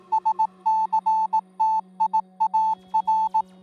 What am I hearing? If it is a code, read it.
Morse code, heard as a keyed tone, about 18 wpm, 880 Hz. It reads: SCTIAR